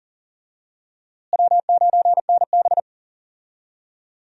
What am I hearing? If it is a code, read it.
Morse code, heard as a keyed tone, about 40 wpm, 700 Hz. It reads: W9NB
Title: USA Call Signs - 4 Characters - Repeated 2x - 40wpm